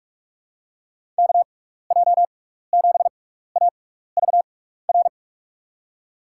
Morse code, heard as a keyed tone, about 46 wpm, spaced as for 16 wpm, 700 Hz. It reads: KJ7AVR
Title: USA Call Signs - 6 Characters - 6x Character Spacing - 45wpm